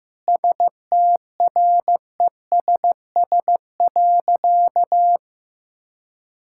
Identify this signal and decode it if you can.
Morse code, heard as a keyed tone, about 16 wpm, 700 Hz. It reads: STRESS.
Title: Sentences from Top 1000 Words - Review Part 6 - 15wpm